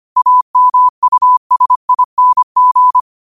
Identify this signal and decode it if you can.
Morse code, heard as a keyed tone, about 25 wpm, 1000 Hz. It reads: AMUSING
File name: 25wpm